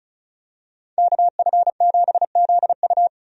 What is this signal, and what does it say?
Morse code, heard as a keyed tone, about 34 wpm, 700 Hz. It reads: KF7ZU
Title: USA Call Signs - 5 Characters - 35wpm